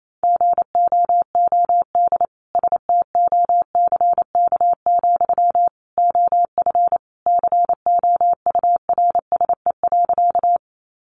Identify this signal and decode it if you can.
Morse code, heard as a keyed tone, about 28 wpm, 700 Hz. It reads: GOOD STOCK, OF COURSE.
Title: Sentences from Top 800 Words 28wpm